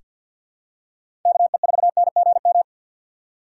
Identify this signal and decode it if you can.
Morse code, heard as a keyed tone, about 47 wpm, 700 Hz. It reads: KE4NGM
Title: USA Call Signs - 6 Characters - 4x Character Spacing - 50wpm